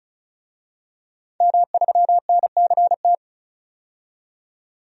Morse code, heard as a keyed tone, about 36 wpm, 700 Hz. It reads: M3NCT
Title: DXCC-100 Call Signs - 5 Characters - 35wpm